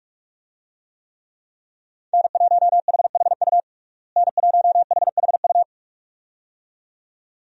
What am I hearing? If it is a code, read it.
Morse code, heard as a keyed tone, about 46 wpm, 700 Hz. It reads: N1HHU N1HHU
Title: USA Call Signs - 5 Characters - Repeated 2x - 45wpm